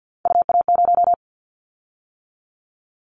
Morse code, heard as a keyed tone, about 50 wpm, 700 Hz. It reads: UA0
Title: Callsign Prefixes for Top 100 DXCC - 50wpm